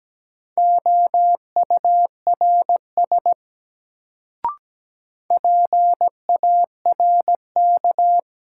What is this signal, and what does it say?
Morse code, heard as a keyed tone, about 17 wpm, 700 Hz. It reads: OURS PARK
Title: Sentences from Top 1300 Words - New Word Drill Alt - 17wpm